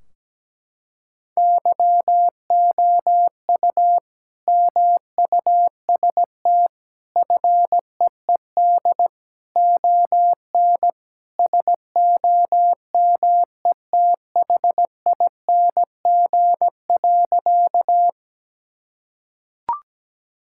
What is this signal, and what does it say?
Morse code, heard as a keyed tone, about 17 wpm, 700 Hz. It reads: YOU MUST FEED ON SOMETHING.
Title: Sentences from Top 1600 Words 17wpm